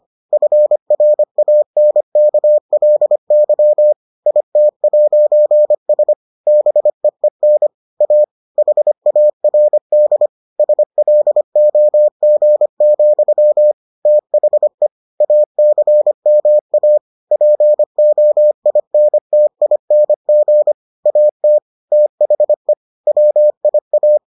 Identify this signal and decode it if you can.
Morse code, heard as a keyed tone, about 25 wpm, 600 Hz. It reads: FRANKLY IT'S BEEN A HARD SLOG, THE ACMA POINTING AT THE WIA